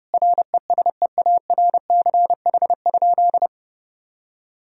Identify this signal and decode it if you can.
Morse code, heard as a keyed tone, about 30 wpm, 700 Hz. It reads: RESEARCH?
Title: Sentences from Top 1000 Words - Review Part 4 - 30wpm